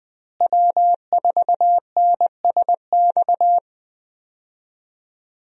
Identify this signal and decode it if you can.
Morse code, heard as a keyed tone, about 20 wpm, 700 Hz. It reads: W4NSX
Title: USA Call Signs - 5 Characters - Repeated 3x - 20wpm